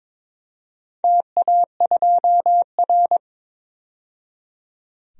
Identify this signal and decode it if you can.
Morse code, heard as a keyed tone, about 22 wpm, 700 Hz. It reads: TA2R
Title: DXCC-100 Call Signs - 4 Characters - 22wpm